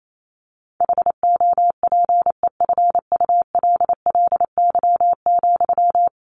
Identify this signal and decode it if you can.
Morse code, heard as a keyed tone, about 28 wpm, 700 Hz. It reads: HOPEFULLY,
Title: Sentences from Top 1500 Words 28wpm